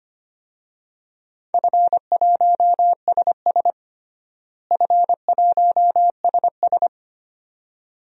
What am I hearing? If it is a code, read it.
Morse code, heard as a keyed tone, about 25 wpm, 700 Hz. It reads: F1SS F1SS
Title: DXCC-100 Call Signs - 4 Characters - Repeated 2x - 25wpm